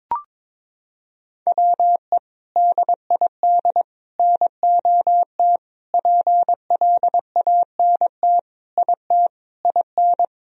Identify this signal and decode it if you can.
Morse code, heard as a keyed tone, about 21 wpm, 700 Hz. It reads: WE DID NOT PLANT IT IN
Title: Sentences from Top 1000 Words - Review Part 1 - 22wpm